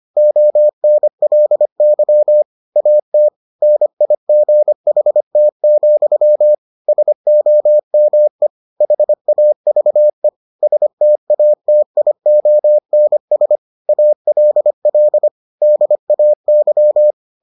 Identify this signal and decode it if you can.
Morse code, heard as a keyed tone, about 25 wpm, 600 Hz. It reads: ONLY AT NIGHT, SOME HAVE STATIONS ALL DAY